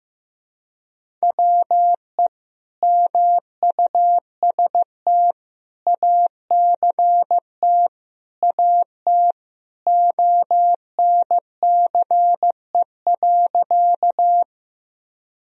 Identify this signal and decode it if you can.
Morse code, heard as a keyed tone, about 16 wpm, 700 Hz. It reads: WE MUST ACT AT ONCE.
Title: Sentences from Top 1000 Words - Review Part 1 - 15wpm